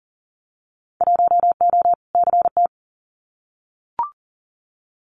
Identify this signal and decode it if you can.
Morse code, heard as a keyed tone, about 38 wpm, 700 Hz. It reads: 1O CT
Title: Winter Field Day Exchanges 40wpm